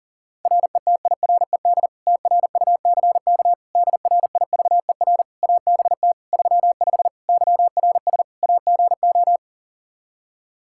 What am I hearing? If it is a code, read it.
Morse code, heard as a keyed tone, about 37 wpm, 700 Hz. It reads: RETIRED TRUCK DRIVER ABT 35 YRS AGO
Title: Practice QSO - Exchange 3 - More Info - 40wpm